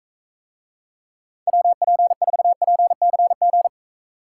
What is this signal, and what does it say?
Morse code, heard as a keyed tone, about 44 wpm, 700 Hz. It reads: WP4PCG